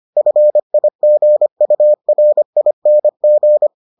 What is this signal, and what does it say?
Morse code, heard as a keyed tone, about 25 wpm, 600 Hz. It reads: FIGURING